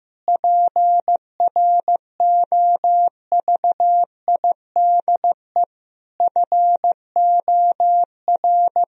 Morse code, heard as a keyed tone, about 15 wpm, 700 Hz. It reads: PROVIDE FOR
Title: Sentences from Top 1000 Words - Review Part 3 - 15wpm